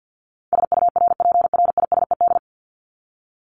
Morse code, heard as a keyed tone, about 48 wpm, 700 Hz. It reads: SURPRISED
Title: Top 1501-1750 Words 50wpm